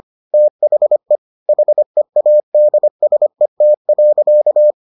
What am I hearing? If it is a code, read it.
Morse code, heard as a keyed tone, about 25 wpm, 600 Hz. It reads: THE HEADSET.